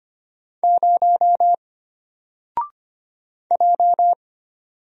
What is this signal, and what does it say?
Morse code, heard as a keyed tone, about 25 wpm, 700 Hz. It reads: 0 J
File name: Single-Letter-Number-25wpm